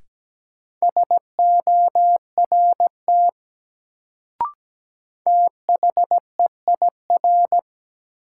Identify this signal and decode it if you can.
Morse code, heard as a keyed tone, about 17 wpm, 700 Hz. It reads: SORT THEIR
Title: Sets of 2 Words - New Word Drill - 17wpm